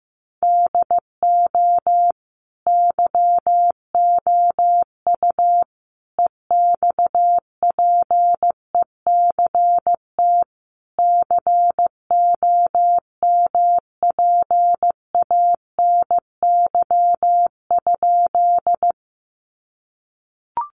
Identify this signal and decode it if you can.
Morse code, heard as a keyed tone, about 15 wpm, 700 Hz. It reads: DO YOU EXPECT COMPANY?
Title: Sentences from Top 1100 Words 15wpm